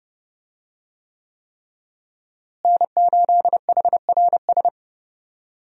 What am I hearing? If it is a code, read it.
Morse code, heard as a keyed tone, about 30 wpm, 700 Hz. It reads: N8HRS
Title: USA Call Signs - 5 Characters - Repeated 3x - 30wpm